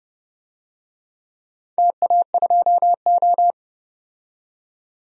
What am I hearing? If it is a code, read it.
Morse code, heard as a keyed tone, about 30 wpm, 700 Hz. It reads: TA2O
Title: DXCC-100 Call Signs - 4 Characters - 30wpm